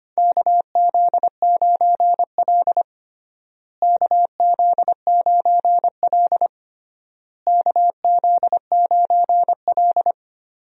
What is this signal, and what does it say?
Morse code, heard as a keyed tone, about 25 wpm, 700 Hz. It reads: KZ9L KZ9L KZ9L
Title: USA Call Signs - 4 Characters - Repeated 3x - 25wpm